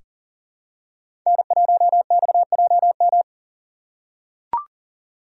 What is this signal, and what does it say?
Morse code, heard as a keyed tone, about 40 wpm, 700 Hz. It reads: N1XJM
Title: K1USN SST QSO Practice - 40wpm